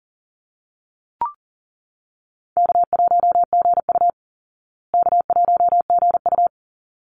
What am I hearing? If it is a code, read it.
Morse code, heard as a keyed tone, about 40 wpm, 700 Hz. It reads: K1GU K1GU
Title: Easy Call Signs Repeated 2x - 40wpm